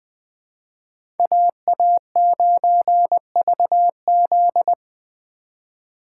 Morse code, heard as a keyed tone, about 20 wpm, 700 Hz. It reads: AA9VZ
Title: USA Call Signs - 5 Characters - 4x Character Spacing - 20wpm